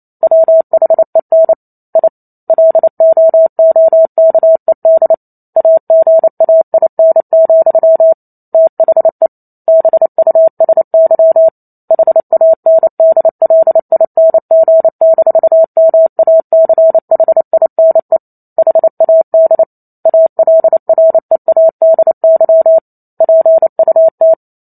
Morse code, heard as a keyed tone, about 28 wpm, 650 Hz. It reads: WHEN I LOOKED AGAIN, THE BUSY HANDLING-MACHINE HAD ALREADY PUT